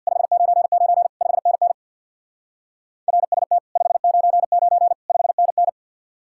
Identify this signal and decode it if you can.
Morse code, heard as a keyed tone, about 66 wpm, 700 Hz. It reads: 599 5NN RST 599 5NN